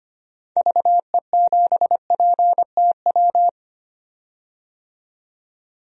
Morse code, heard as a keyed tone, about 25 wpm, 700 Hz. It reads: VE7PTW
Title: DXCC-100 Call Signs - 6 Characters - 25wpm